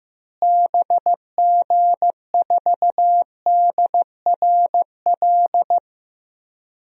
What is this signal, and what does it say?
Morse code, heard as a keyed tone, about 15 wpm, 700 Hz. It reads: BG4DRL
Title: DXCC-100 Call Signs - 6 Characters - Repeated 3x - 15wpm